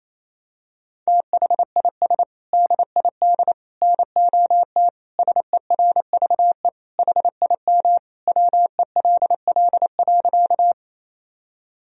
Morse code, heard as a keyed tone, about 28 wpm, 700 Hz. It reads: THIS DID NOT SERVE HIM WELL.
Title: Sentences from Top 1000 Words - Review Part 2 - 28wpm